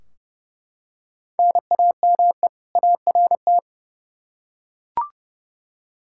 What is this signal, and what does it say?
Morse code, heard as a keyed tone, about 32 wpm, 700 Hz. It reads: NAME ART
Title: K1USN SST QSO Practice 30wpm